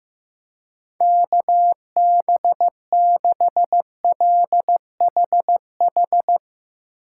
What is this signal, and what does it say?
Morse code, heard as a keyed tone, about 15 wpm, 700 Hz. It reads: KB6LHH
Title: USA Call Signs - 6 Characters - 2x Character Spacing - 15wpm